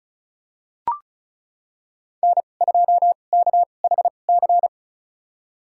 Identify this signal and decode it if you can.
Morse code, heard as a keyed tone, about 36 wpm, spaced as for 25 wpm, 700 Hz. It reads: N2KHC
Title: USA Call Signs - 2x Character Spacing - 35wpm